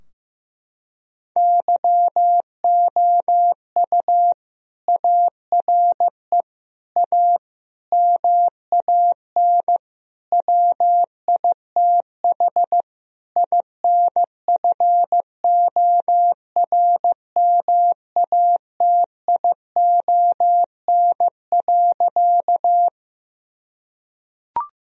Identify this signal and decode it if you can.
Morse code, heard as a keyed tone, about 15 wpm, 700 Hz. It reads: YOU ARE A MAN WITH INFORMATION.
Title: Sentences from Top 1000 Words - Review Part 3 - 15wpm